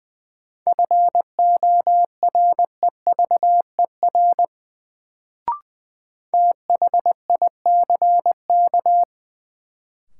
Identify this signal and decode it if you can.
Morse code, heard as a keyed tone, about 20 wpm, 700 Hz. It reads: FOREVER THICK
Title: Sentences from Top 1600 Words - New Word Drill Alt - 20wpm